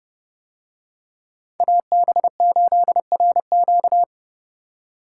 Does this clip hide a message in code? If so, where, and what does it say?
Yes: Morse code from 1.6 to 4.0 s, heard as a keyed tone, about 30 wpm, 700 Hz. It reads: AB8RQ